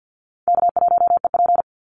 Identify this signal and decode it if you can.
Morse code, heard as a keyed tone, about 50 wpm, 700 Hz. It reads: K1EP
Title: Call Signs - 50wpm